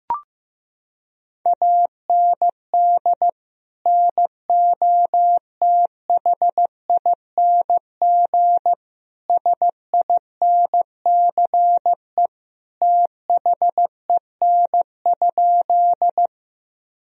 Morse code, heard as a keyed tone, about 15 wpm, 700 Hz. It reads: AND NOTHING SINCE THEN?
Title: Sentences from Top 1000 Words - Review Part 1 - 15wpm